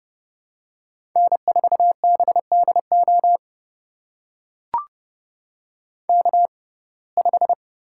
Morse code, heard as a keyed tone, about 30 wpm, spaced as for 22 wpm, 700 Hz. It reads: N4BDO K5
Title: USA Call Signs - 5 Characters - 6x Character Spacing - 30wpm